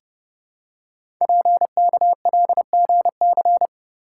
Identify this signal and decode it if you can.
Morse code, heard as a keyed tone, about 29 wpm, 700 Hz. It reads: PKLGC
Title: Code Groups - 5 Letter:Numbers - 30wpm